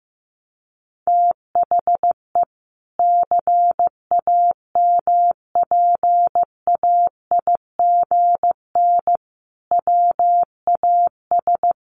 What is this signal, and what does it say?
Morse code, heard as a keyed tone, about 15 wpm, 700 Hz. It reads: THE CAMPAIGN WAS
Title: Sentences from Top 1200 Words 15wpm